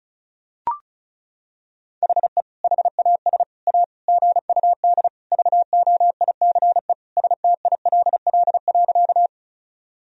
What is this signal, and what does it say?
Morse code, heard as a keyed tone, about 36 wpm, 700 Hz. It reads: HE HAS A GUD VOICE STILL.
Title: Sentences from Top 500 Words and Abbreviations 35wpm